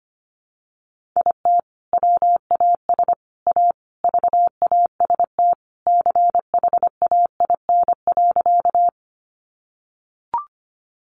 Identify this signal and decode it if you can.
Morse code, heard as a keyed tone, about 25 wpm, 700 Hz. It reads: IT WAS A VAST CHAIN.
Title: Sentences from Top 1500 Words 25wpm